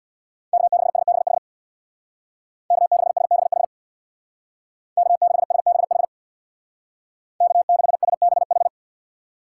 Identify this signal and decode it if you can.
Morse code, heard as a keyed tone, about 64 wpm, 700 Hz. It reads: K6SBH K6SBH K6SBH K6SBH